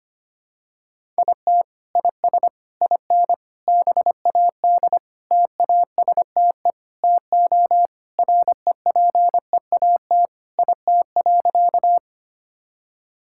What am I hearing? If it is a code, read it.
Morse code, heard as a keyed tone, about 25 wpm, 700 Hz. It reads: IT IS IN BAD TASTE TO REPEAT IT.
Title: Sentences from Top 1600 Words 25wpm